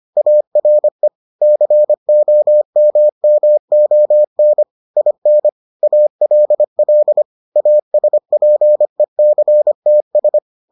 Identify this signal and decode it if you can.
Morse code, heard as a keyed tone, about 25 wpm, 600 Hz. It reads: ARE COMMON IN ALL ASPECTS